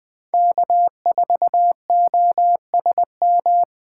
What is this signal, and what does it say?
Morse code, heard as a keyed tone, about 25 wpm, 700 Hz. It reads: K4OSM
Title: Speed Racing - USA 1x3 Call Signs 20wpm